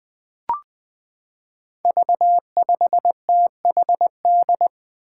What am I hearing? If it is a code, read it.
Morse code, heard as a keyed tone, about 20 wpm, 700 Hz. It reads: V5THD
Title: Five Challenging Letter-Number Combinations 20wpm